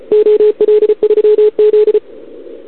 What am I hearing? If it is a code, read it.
Morse code, heard as a keyed tone, about 36 wpm, 440 Hz, with radio noise. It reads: OL3Z
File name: CQWWCW2021 / OL3Z